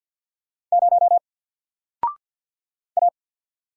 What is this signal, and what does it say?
Morse code, heard as a keyed tone, about 48 wpm, 700 Hz. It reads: 0 A
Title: Single Letter-Number Alt 50wpm